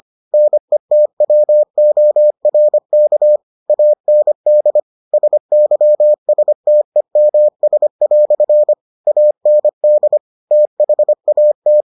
Morse code, heard as a keyed tone, about 25 wpm, 600 Hz. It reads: NETWORK AND SYSTEMS" AND THAT